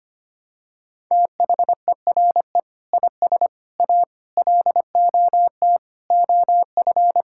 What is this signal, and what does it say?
Morse code, heard as a keyed tone, about 25 wpm, 700 Hz. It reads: THERE IS A LOT OF
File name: practice / 25wpm / Sentences with 2-Word Phrases - 25wpm